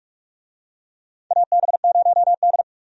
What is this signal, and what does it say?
Morse code, heard as a keyed tone, about 47 wpm, 700 Hz. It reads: AB0D
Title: USA Call Signs - 4 Characters - Repeated 2x - 45wpm